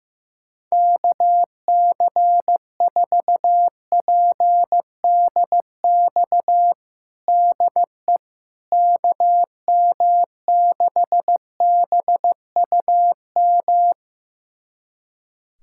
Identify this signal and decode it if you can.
Morse code, heard as a keyed tone, about 15 wpm, 700 Hz. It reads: KC4PDX DE KM6BUM
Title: USA Callsign DE Callsign 15wpm